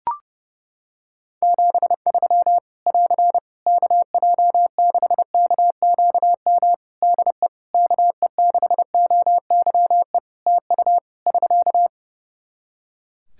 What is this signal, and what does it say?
Morse code, heard as a keyed tone, about 30 wpm, 700 Hz. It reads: 73 <AR> KJ6KQM DE KE6OYE TU <SK>
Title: Practice QSO - Exchange 4 - Ending - 30wpm